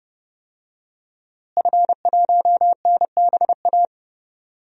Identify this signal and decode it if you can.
Morse code, heard as a keyed tone, about 30 wpm, 700 Hz. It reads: F1NBA
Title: DXCC-100 Call Signs - 5 Characters - Repeated 2x - 30wpm